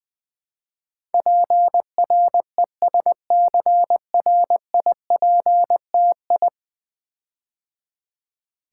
Morse code, heard as a keyed tone, about 20 wpm, 700 Hz. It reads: PRESCRIPTI
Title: Build A Word - 5 Letters At A Time - 12-Letter Words - 20wpm